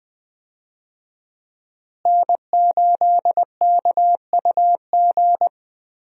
Morse code, heard as a keyed tone, about 20 wpm, 700 Hz. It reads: N8KUG
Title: USA Call Signs - 5 Characters - Repeated 3x - 20wpm